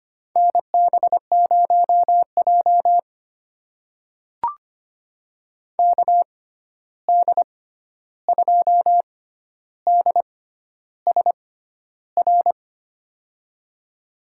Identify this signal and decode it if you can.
Morse code, heard as a keyed tone, about 25 wpm, spaced as for 11 wpm, 700 Hz. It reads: NB0J KD2DSR
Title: USA Call Signs - 6x Character Spacing - 25wpm